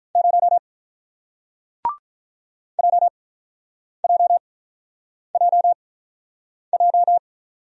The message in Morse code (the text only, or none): 0 JJJJ